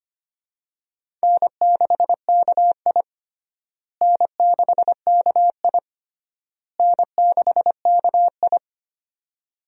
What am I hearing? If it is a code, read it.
Morse code, heard as a keyed tone, about 25 wpm, 700 Hz. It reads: N6KI N6KI N6KI
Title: USA Call Signs - 4 Characters - Repeated 3x - 25wpm